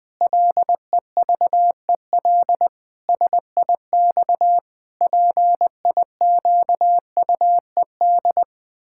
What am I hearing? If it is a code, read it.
Morse code, heard as a keyed tone, about 20 wpm, 700 Hz. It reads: LEVEL SIX PIQUED
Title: Pangram Challenge v2 - 20wpm